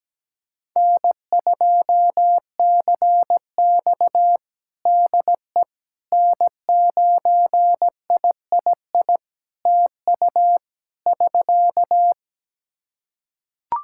Morse code, heard as a keyed tone, about 17 wpm, 700 Hz. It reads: N2CX DE N9III TU <SK>
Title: K1USN SST QSO Practice 17wpm